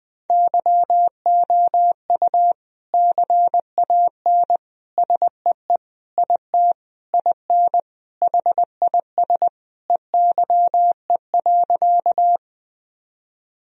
Morse code, heard as a keyed tone, about 20 wpm, 700 Hz. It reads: YOU CAN SEE IT IN HIS EYE.
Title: Sentences from Top 1000 Words - Review Part 1 - 20wpm